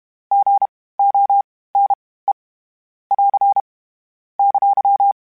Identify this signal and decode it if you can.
Morse code, heard as a keyed tone, about 32 wpm, spaced as for 18 wpm, 800 Hz. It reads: GONE <AR> !